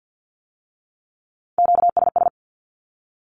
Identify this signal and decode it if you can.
Morse code, heard as a keyed tone, about 50 wpm, 700 Hz. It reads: QSS
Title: Q-Codes with Spoken Meaning - 50wpm